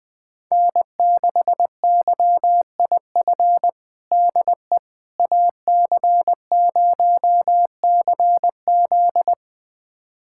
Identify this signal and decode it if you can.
Morse code, heard as a keyed tone, about 20 wpm, 700 Hz. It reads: N6YIF DE AC0CZ
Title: USA Callsign DE Callsign - 5 Characters - 20wpm